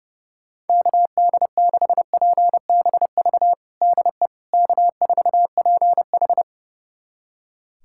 Morse code, heard as a keyed tone, about 30 wpm, 700 Hz. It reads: KD6PBV DE K4PH
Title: USA Callsign DE Callsign 30wpm